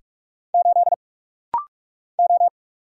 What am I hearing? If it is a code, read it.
Morse code, heard as a keyed tone, about 46 wpm, 700 Hz. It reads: 8 O
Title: Single Letter-Number Alt - Rapid-Fire 45wpm